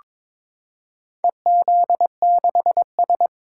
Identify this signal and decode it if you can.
Morse code, heard as a keyed tone, about 22 wpm, 700 Hz. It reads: EZ6S
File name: practice / 22wpm / Four Letter-Number Combinations - 22wpm